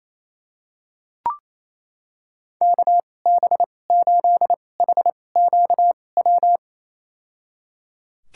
Morse code, heard as a keyed tone, about 27 wpm, spaced as for 20 wpm, 700 Hz. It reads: KB8HQW
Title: USA Call Signs - 6 Characters - 2x Character Spacing - 28wpm